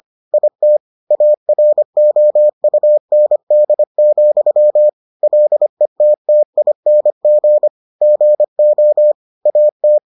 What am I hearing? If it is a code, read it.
Morse code, heard as a keyed tone, about 25 wpm, 600 Hz. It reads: IT AROUND, LETTING GO AT